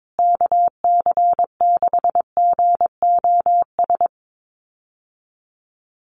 Morse code, heard as a keyed tone, about 22 wpm, 700 Hz. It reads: KC6GOS
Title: USA Call Signs Repeated 3x 22wpm